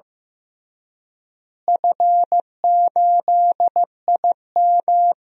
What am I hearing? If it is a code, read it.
Morse code, heard as a keyed tone, about 16 wpm, 700 Hz. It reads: F8IM
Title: DXCC-100 Call Signs - 4 Characters - Repeated 3x - 15wpm